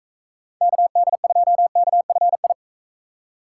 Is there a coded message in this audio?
Yes: Morse code from 0.6 to 2.5 s, heard as a keyed tone, about 44 wpm, 700 Hz. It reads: KD2KFI